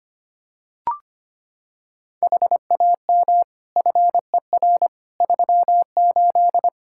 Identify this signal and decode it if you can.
Morse code, heard as a keyed tone, about 25 wpm, 700 Hz. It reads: HAM FER 38
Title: K1USN SST QSO Practice 25wpm